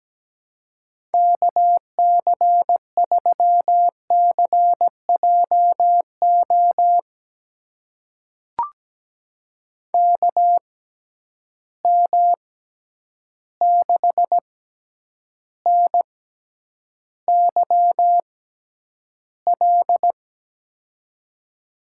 Morse code, heard as a keyed tone, about 17 wpm, spaced as for 9 wpm, 700 Hz. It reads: KC3CJO KM6NYL
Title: USA Call Signs - 6 Characters - 6x Character Spacing - 17wpm